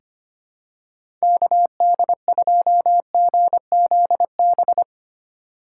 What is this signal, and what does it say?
Morse code, heard as a keyed tone, about 25 wpm, 700 Hz. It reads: KD2GZB
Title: USA Call Signs - 6 Characters - 3x Character Spacing - 25wpm